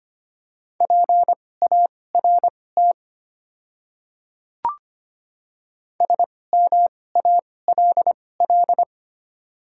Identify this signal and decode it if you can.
Morse code, heard as a keyed tone, about 25 wpm, spaced as for 18 wpm, 690 Hz. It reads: PART SMALL
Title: 2x Character Spacing - Top 200 Words - 25wpm